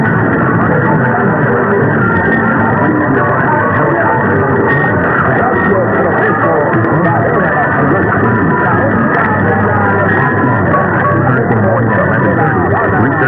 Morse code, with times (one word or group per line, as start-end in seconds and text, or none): none